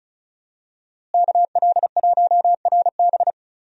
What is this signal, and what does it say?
Morse code, heard as a keyed tone, about 36 wpm, 700 Hz. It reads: KL1RB
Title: USA Call Signs - 5 Characters - Repeated 3x - 35wpm